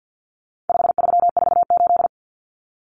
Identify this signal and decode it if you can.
Morse code, heard as a keyed tone, about 50 wpm, 700 Hz. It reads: 5348
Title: Common Cut Numbers - 4-Digit 50wpm